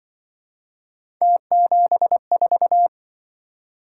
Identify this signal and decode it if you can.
Morse code, heard as a keyed tone, about 25 wpm, 700 Hz. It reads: T74